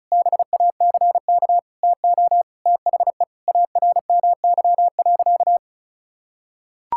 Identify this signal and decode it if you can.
Morse code, heard as a keyed tone, about 36 wpm, 700 Hz. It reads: BACK TO THE ARMY.
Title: Sentences from Top 1000 Words - Review Part 6 - 35wpm